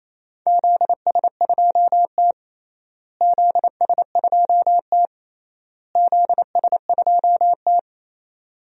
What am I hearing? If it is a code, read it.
Morse code, heard as a keyed tone, about 27 wpm, 700 Hz. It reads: ZS2T ZS2T ZS2T
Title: DXCC-100 Call Signs - 4 Characters - Repeated 3x - 28wpm